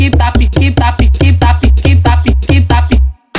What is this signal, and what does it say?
There is no code.